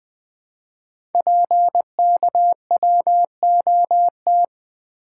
Morse code, heard as a keyed tone, about 20 wpm, 700 Hz. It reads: PKWOT
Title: Code Groups - 5 Letters - 20wpm